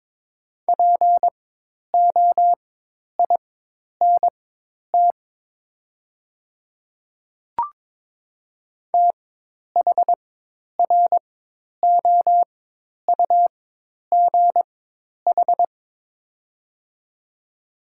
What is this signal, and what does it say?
Morse code, heard as a keyed tone, about 21 wpm, spaced as for 10 wpm, 700 Hz. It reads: POINT THROUGH
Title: 4x Character Spacing - Top 300 Words - 22wpm